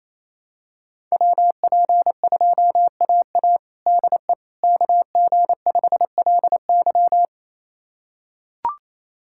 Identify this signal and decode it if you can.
Morse code, heard as a keyed tone, about 28 wpm, 700 Hz. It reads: WP2AA DE KG5LY
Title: USA Callsign DE Callsign - 5 Characters - 28wpm